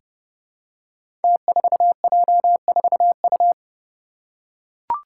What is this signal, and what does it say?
Morse code, heard as a keyed tone, about 29 wpm, 700 Hz. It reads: T4J4U
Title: Five Challenging Letter-Number Combinations 30wpm